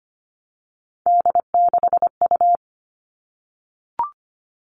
Morse code, heard as a keyed tone, about 25 wpm, 700 Hz. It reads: D6U
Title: All Cut Numbers - 3-Digit 25wpm